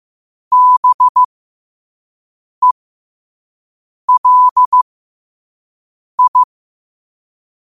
Morse code, heard as a keyed tone, about 15 wpm, spaced as for 5 wpm, 1000 Hz. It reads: BELI
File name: qotdfiles / 5wpm